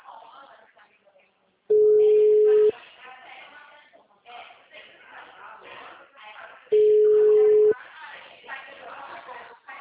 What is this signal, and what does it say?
Ringback tone (Europe, 425 Hz), with speech.
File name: out- -2010- - - .54087